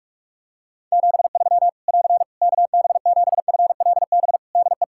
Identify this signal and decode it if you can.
Morse code, heard as a keyed tone, about 46 wpm, 700 Hz. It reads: 73 <AR> KB7FLB DE
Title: Practice QSO - Exchange 4 - Ending - 45wpm